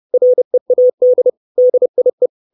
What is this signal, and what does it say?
Morse code, heard as a keyed tone, about 30 wpm, 500 Hz. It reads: READ DIE
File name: 632 Words 30WPM